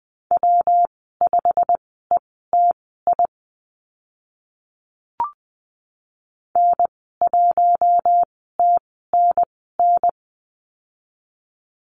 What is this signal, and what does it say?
Morse code, heard as a keyed tone, about 20 wpm, spaced as for 14 wpm, 700 Hz. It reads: W5ETI N1TNN
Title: 140 Lesson 23g - Letter f - Call Signs 20wpm